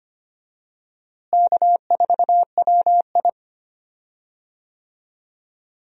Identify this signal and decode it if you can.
Morse code, heard as a keyed tone, about 25 wpm, 700 Hz. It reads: K4WI
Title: USA Call Signs - 4 Characters - Repeated 2x - 25wpm